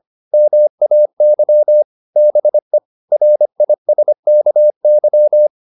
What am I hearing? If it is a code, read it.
Morse code, heard as a keyed tone, about 25 wpm, 600 Hz. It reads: MAY BE RISKY